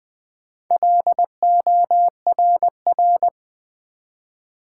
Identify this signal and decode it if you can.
Morse code, heard as a keyed tone, about 20 wpm, 700 Hz. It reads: LORR
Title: Build A Word - 1 Letter At A Time - 5-Letter Words - 20wpm